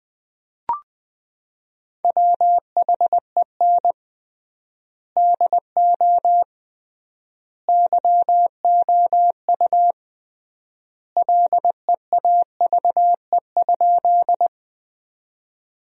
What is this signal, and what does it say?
Morse code, heard as a keyed tone, about 21 wpm, 700 Hz. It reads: WHEN DO YOU LEAVE?
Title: 3x Word-Spacing - Sentences from Top 400 Words - 20wpm